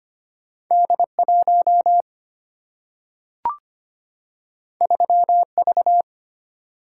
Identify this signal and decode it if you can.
Morse code, heard as a keyed tone, about 25 wpm, 700 Hz. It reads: D 1 3 V
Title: Single-Letter Single-Number 25wpm